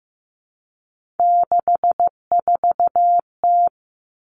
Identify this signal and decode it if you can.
Morse code, heard as a keyed tone, about 15 wpm, 700 Hz. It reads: 64T
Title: Common Cut Numbers - 3-Digit 15wpm